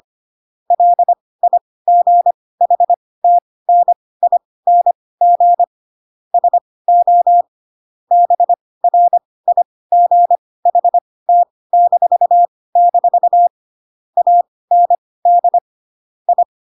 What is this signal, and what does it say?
Morse code, heard as a keyed tone, about 25 wpm, spaced as for 18 wpm, 700 Hz. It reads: LIGHTNING SO BRIGHT-- AND I
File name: Pinocchio / Pinocchio ch7 12 25 ik1hge